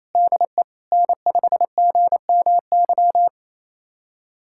Morse code, heard as a keyed tone, about 28 wpm, 700 Hz. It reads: DE N5GMY
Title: USA Callsign DE Callsign - 5 Characters - 28wpm